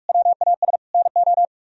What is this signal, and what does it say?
Morse code, heard as a keyed tone, about 42 wpm, 700 Hz. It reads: WAS NO